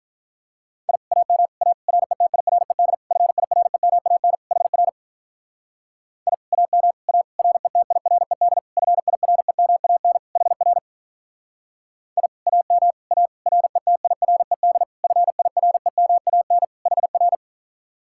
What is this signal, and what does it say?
Morse code, heard as a keyed tone, about 44 wpm, 700 Hz. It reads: I AM A RETIRED FIREMAN HR I AM A RETIRED FIREMAN HR I AM A RETIRED FIREMAN HR